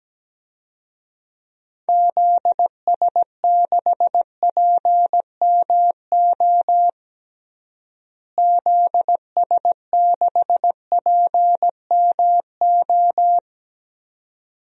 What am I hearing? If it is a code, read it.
Morse code, heard as a keyed tone, about 17 wpm, 700 Hz. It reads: ZS6PMO ZS6PMO
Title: DXCC-100 Call Signs - 6 Characters - Repeated 2x - 17wpm